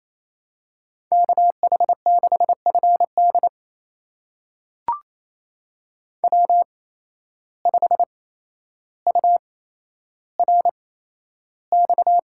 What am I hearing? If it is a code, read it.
Morse code, heard as a keyed tone, about 28 wpm, spaced as for 12 wpm, 700 Hz. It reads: KH6FD W5URX
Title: USA Call Signs - 5 Characters - 8x Character Spacing - 28wpm